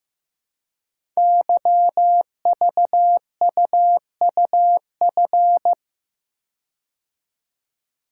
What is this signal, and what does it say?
Morse code, heard as a keyed tone, about 15 wpm, 700 Hz. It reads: YVUUF
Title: Code Groups - 5 Letter:Numbers - 15wpm